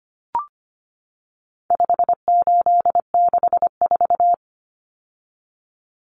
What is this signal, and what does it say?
Morse code, heard as a keyed tone, about 25 wpm, 700 Hz. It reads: 5864
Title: Numbers - 4-Digit 25wpm